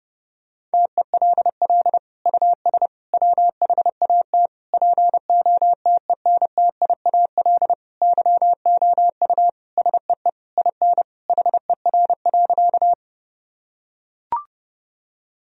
Morse code, heard as a keyed tone, about 30 wpm, 700 Hz. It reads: TELL US WHAT POTENTIAL YOU SEE IN HER.
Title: Sentences from Top 1000 Words - Review Part 4 - 30wpm